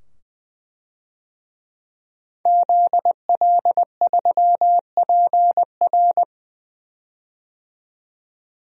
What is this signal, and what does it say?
Morse code, heard as a keyed tone, about 20 wpm, 700 Hz. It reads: ZL3PR
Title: DXCC-100 Call Signs - 5 Characters - 20wpm